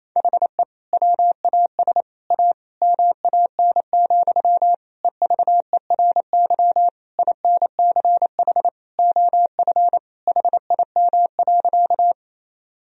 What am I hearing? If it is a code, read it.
Morse code, heard as a keyed tone, about 28 wpm, 700 Hz. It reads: HE WAS A MAN, EVERY INCH OF HIM.
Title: Sentences from Top 1000 Words - Review Part 3 - 28wpm